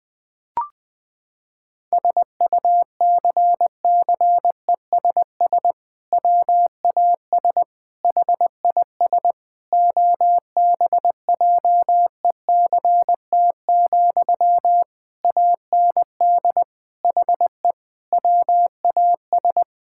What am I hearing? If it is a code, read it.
Morse code, heard as a keyed tone, about 19 wpm, 700 Hz. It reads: SUCCESS WAS HIS OBJECT, AND HE WAS
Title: Sentences from Top 1000 Words - Review Part 6 - 20wpm